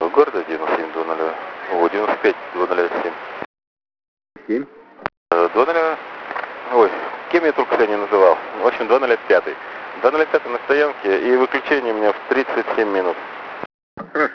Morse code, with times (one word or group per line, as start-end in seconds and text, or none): none